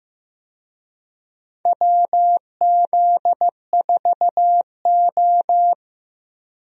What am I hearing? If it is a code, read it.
Morse code, heard as a keyed tone, about 15 wpm, 700 Hz. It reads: WZ4O
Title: USA Call Signs - 4 Characters - Repeated 2x - 15wpm